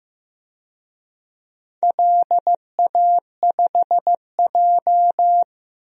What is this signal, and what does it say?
Morse code, heard as a keyed tone, about 15 wpm, 700 Hz. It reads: LA5J
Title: DXCC-100 Call Signs - 4 Characters - 15wpm